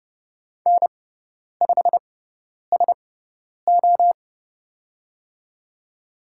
Morse code, heard as a keyed tone, about 30 wpm, spaced as for 10 wpm, 700 Hz. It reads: N5SO
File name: Lesson 17g - Letter u - Call Signs 30wpm